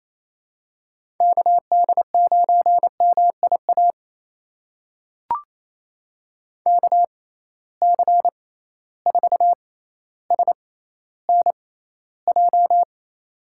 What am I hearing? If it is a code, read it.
Morse code, heard as a keyed tone, about 28 wpm, spaced as for 14 wpm, 700 Hz. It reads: KD9MIA KC4SNJ
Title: USA Call Signs - 6 Characters - 6x Character Spacing - 28wpm